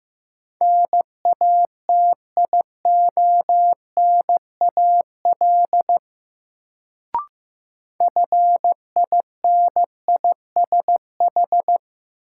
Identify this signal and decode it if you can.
Morse code, heard as a keyed tone, about 15 wpm, 700 Hz. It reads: NATIONAL FINISH
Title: Sentences from Top 1200 Words - New Word Drill Alt - 15wpm